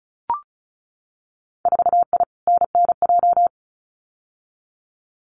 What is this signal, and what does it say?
Morse code, heard as a keyed tone, about 34 wpm, 700 Hz. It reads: 4I NNJ
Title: Winter Field Day Exchanges 35wpm